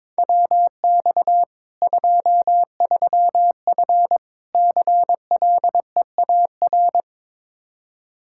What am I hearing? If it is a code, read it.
Morse code, heard as a keyed tone, about 22 wpm, 700 Hz. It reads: WX 23F CLEAR
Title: K1USN SST QSO Practice 22wpm